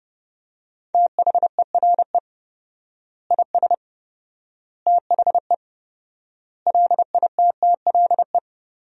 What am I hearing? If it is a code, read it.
Morse code, heard as a keyed tone, about 30 wpm, 700 Hz. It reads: THERE IS THE LITTLE
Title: 4x Word-Spacing - Sentences from Top 200 Words - 30wpm